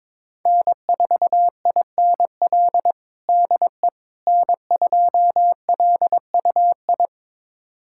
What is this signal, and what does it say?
Morse code, heard as a keyed tone, about 21 wpm, 700 Hz. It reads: N4INL DE N2LUI
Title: USA Callsign DE Callsign - 5 Characters - 22wpm